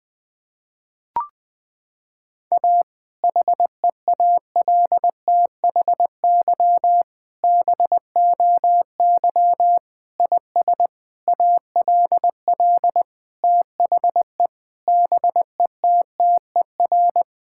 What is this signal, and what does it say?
Morse code, heard as a keyed tone, about 20 wpm, 700 Hz. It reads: A HEALTHY BOY IS ALL THE BETTER
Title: Sentences from Top 1000 Words - Review Part 5 - 20wpm